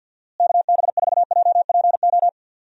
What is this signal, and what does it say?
Morse code, heard as a keyed tone, about 49 wpm, 700 Hz. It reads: KB4JPO
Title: USA Call Signs - 6 Characters - Repeated 3x - 50wpm